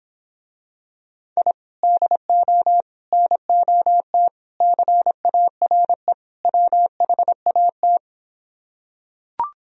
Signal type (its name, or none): Morse code